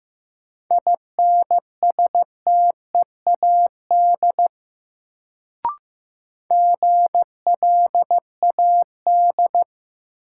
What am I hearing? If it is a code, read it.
Morse code, heard as a keyed tone, about 15 wpm, 700 Hz. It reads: INSTEAD GLAD
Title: Sentences from Top 800 Words - New Word Drill - 15wpm